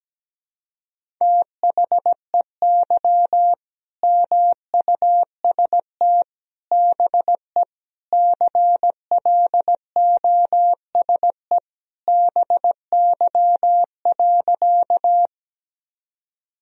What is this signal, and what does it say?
Morse code, heard as a keyed tone, about 17 wpm, 700 Hz. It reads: THEY MUST BE CLOSE BY.
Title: Sentences from Top 1000 Words - Review Part 1 - 17wpm